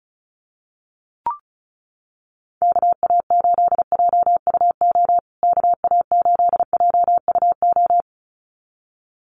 Morse code, heard as a keyed tone, about 34 wpm, 700 Hz. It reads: KA8JUO KA8JUO
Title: USA Call Signs Repeated 2x 35wpm